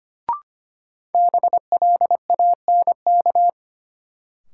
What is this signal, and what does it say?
Morse code, heard as a keyed tone, about 25 wpm, 700 Hz. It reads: BLANK
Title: Sentences from Top 1500 Words - New Word Drill - 25wpm